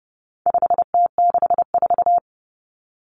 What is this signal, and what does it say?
Morse code, heard as a keyed tone, about 30 wpm, 700 Hz. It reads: 5T64
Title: Common Cut Numbers - 4-Digit 30wpm